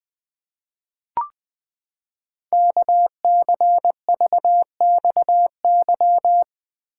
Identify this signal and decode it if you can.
Morse code, heard as a keyed tone, about 20 wpm, 700 Hz. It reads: KCVXY
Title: Code Groups - 5 Letters - 20wpm